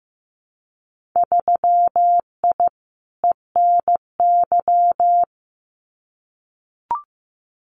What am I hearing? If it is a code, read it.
Morse code, heard as a keyed tone, about 14 wpm, 700 Hz. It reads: 3I ENY
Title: Winter Field Day Exchanges Encore 15wpm